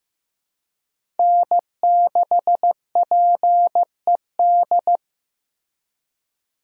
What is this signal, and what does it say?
Morse code, heard as a keyed tone, about 16 wpm, 700 Hz. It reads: N6PED
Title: USA Call Signs - 5 Characters - Repeated 2x - 15wpm